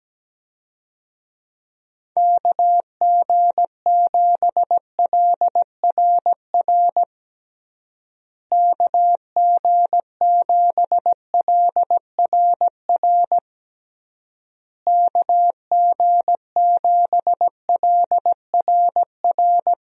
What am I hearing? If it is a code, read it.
Morse code, heard as a keyed tone, about 17 wpm, 700 Hz. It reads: KG7LRR KG7LRR KG7LRR
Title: USA Call Signs - 6 Characters - Repeated 3x - 17wpm